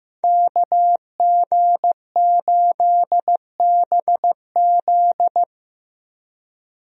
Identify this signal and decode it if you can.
Morse code, heard as a keyed tone, about 15 wpm, 700 Hz. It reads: KG8BZ
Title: USA Call Signs - 5 Characters - Repeated 2x - 15wpm